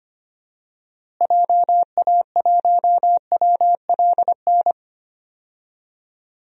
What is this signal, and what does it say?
Morse code, heard as a keyed tone, about 25 wpm, 700 Hz. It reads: JA1WLN
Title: DXCC-100 Call Signs - 6 Characters - Repeated 2x - 25wpm